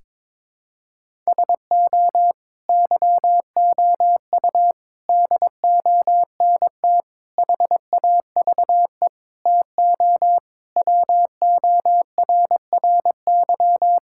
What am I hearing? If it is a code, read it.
Morse code, heard as a keyed tone, about 22 wpm, 700 Hz. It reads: SO YOU DONT HAVE TO WORRY
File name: practice / 22wpm / Sets of 7 Words - 22wpm